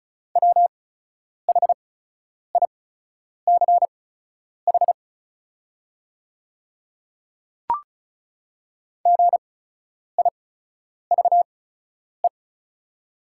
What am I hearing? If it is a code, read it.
Morse code, heard as a keyed tone, about 34 wpm, spaced as for 10 wpm, 700 Hz. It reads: WHICH GIVE
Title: 8x Character Spacing - Top 300 Words - 35wpm